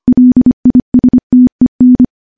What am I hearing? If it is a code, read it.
Morse code, heard as a keyed tone, about 25 wpm, 250 Hz. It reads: LISTEN
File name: morse-training-25WPM